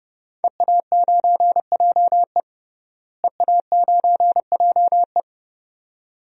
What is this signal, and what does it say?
Morse code, heard as a keyed tone, about 30 wpm, 700 Hz. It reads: EA9JE EA9JE
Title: DXCC-100 Call Signs - 5 Characters - Repeated 2x - 30wpm